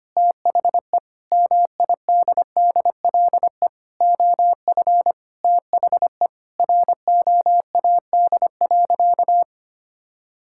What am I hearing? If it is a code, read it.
Morse code, heard as a keyed tone, about 25 wpm, 700 Hz. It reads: THE MIDDLE OF THE ROAD.
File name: practice / 25wpm / Sentences with 3-Word Phrases - 25wpm